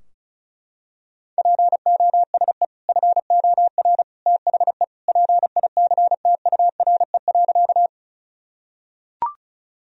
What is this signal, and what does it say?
Morse code, heard as a keyed tone, about 36 wpm, 700 Hz. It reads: POSE FOR THE PICTURE.
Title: Sentences from Top 1000 Words - Review Part 2 - 35wpm